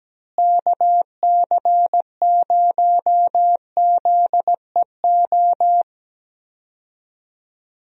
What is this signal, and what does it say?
Morse code, heard as a keyed tone, about 17 wpm, 700 Hz. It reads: KC0ZEO
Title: USA Call Signs - 6 Characters - Repeated 2x - 17wpm